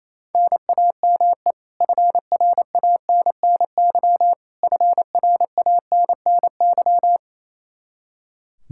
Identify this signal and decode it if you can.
Morse code, heard as a keyed tone, about 27 wpm, 700 Hz. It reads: NAME FRANNY FRANNY
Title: Practice QSO - Exchange 1 - Beginning - 28wpm